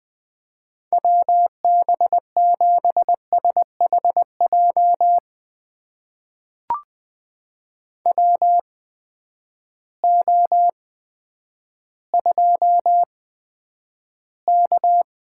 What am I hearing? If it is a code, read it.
Morse code, heard as a keyed tone, about 20 wpm, spaced as for 10 wpm, 700 Hz. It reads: WB7SHJ WO2K
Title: USA Call Signs - 8x Character Spacing - 20wpm